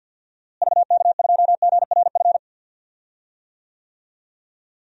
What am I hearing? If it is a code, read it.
Morse code, heard as a keyed tone, about 52 wpm, 700 Hz. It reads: VK2GRF
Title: DXCC-100 Call Signs - 6 Characters - Repeated 3x - 50wpm